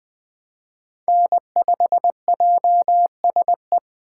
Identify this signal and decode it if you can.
Morse code, heard as a keyed tone, about 20 wpm, 700 Hz. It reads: N5JSE